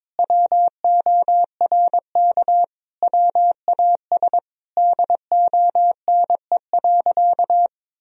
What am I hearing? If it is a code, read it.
Morse code, heard as a keyed tone, about 22 wpm, 700 Hz. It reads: WORK WAS DONE.
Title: Sentences from Top 1000 Words - Review Part 1 - 22wpm